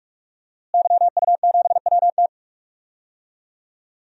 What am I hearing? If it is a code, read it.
Morse code, heard as a keyed tone, about 46 wpm, 700 Hz. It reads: YU7WT
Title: DXCC-100 Call Signs - 5 Characters - Repeated 2x - 45wpm